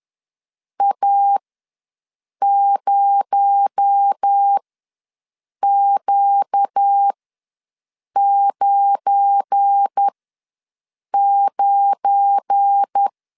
Morse code, heard as a keyed tone, about 10 wpm, spaced as for 6 wpm, 800 Hz. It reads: A0Q99
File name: LEZIONI / 7 GF9Z / 7 GF9 ws 10wpm 4034